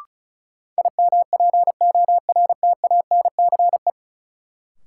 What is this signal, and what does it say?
Morse code, heard as a keyed tone, about 36 wpm, 700 Hz. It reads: IMPORTANCE
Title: Sentences from Top 1200 Words - New Word Drill - 35wpm